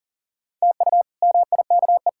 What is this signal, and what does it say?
Morse code, heard as a keyed tone, about 40 wpm, 700 Hz. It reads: TU MIKE
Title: K1USN SST QSO Practice - 40wpm